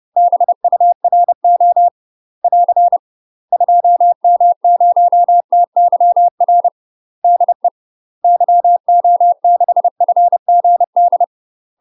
Morse code, heard as a keyed tone, about 30 wpm, 700 Hz. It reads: BURO <AR> 2M0TYR DE YO6FGD
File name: UK MASTER QSOs TRAINING/30WPM